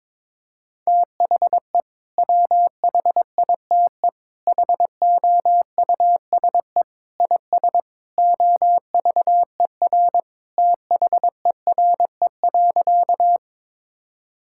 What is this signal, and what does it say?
Morse code, heard as a keyed tone, about 23 wpm, 700 Hz. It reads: THE WHITE HOUSE IS OVER THERE.
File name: practice / 22wpm / Sentences with 3-Word Phrases - 22wpm